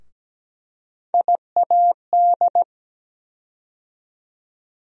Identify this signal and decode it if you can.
Morse code, heard as a keyed tone, about 17 wpm, 700 Hz. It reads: IAD
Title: USA Call Sign Postfixes - 3 Letters 17wpm